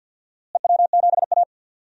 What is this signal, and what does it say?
Morse code, heard as a keyed tone, about 52 wpm, 700 Hz. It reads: EW7A
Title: DXCC-100 Call Signs - 4 Characters - Repeated 2x - 50wpm